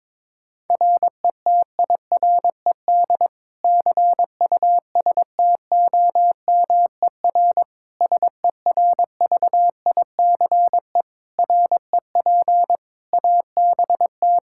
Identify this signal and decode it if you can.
Morse code, heard as a keyed tone, about 22 wpm, 700 Hz. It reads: RETIRED CUSTOMER SERVICE REP ABT
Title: Practice QSO - Exchange 3 - More Info - 22wpm